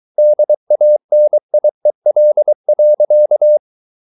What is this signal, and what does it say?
Morse code, heard as a keyed tone, about 23 wpm, 600 Hz. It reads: DANIEL.
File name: Ferie QRS 23wpm